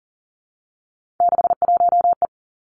Morse code, heard as a keyed tone, about 41 wpm, 700 Hz. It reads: 61E
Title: Three Challenging Letter-Number Combinations 40wpm